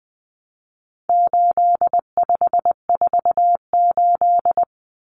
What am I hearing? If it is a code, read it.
Morse code, heard as a keyed tone, about 20 wpm, 700 Hz. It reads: 8548
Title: Numbers - 4-Digit 20wpm